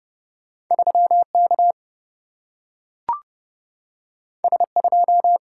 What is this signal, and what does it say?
Morse code, heard as a keyed tone, about 30 wpm, 700 Hz. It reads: 3K S2
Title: Single-Letter Single-Number 30wpm